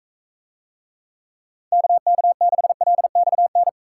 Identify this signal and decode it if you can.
Morse code, heard as a keyed tone, about 45 wpm, 700 Hz. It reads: KK6LXN